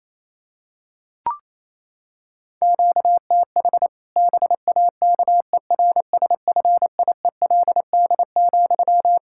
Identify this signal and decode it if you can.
Morse code, heard as a keyed tone, about 28 wpm, 700 Hz. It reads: QTH BAKERSFIELD,
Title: Practice QSO - Exchange 1 - Beginning - 28wpm